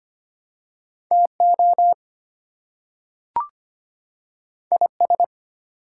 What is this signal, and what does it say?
Morse code, heard as a keyed tone, about 25 wpm, 700 Hz. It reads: TO IS
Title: Two Letter Words - 25wpm